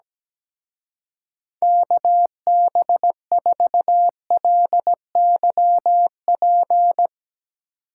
Morse code, heard as a keyed tone, about 17 wpm, 700 Hz. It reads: KB4LYP
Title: Speed Racing - USA 2x3 Call Signs 17wpm